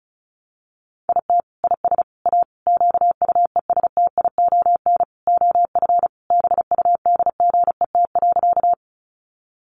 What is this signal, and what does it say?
Morse code, heard as a keyed tone, about 34 wpm, 700 Hz. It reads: IT IS A QUESTION OF BUDGET.
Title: Sentences from Top 1200 Words 35wpm